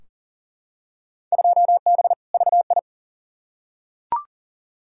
Morse code, heard as a keyed tone, about 40 wpm, 700 Hz. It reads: 2B VI
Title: Field Day Exchanges Encore - 40wpm